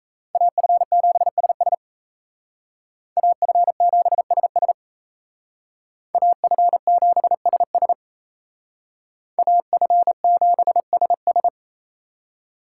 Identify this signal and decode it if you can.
Morse code, heard as a keyed tone, about 35 wpm, 700 Hz. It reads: AF7SS AF7SS AF7SS AF7SS